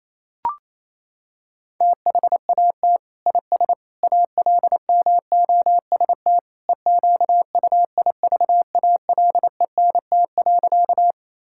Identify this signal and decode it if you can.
Morse code, heard as a keyed tone, about 28 wpm, 700 Hz. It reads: THAT IS ALMOST EQUIVALENT.
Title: Sentences from Top 1600 Words 28wpm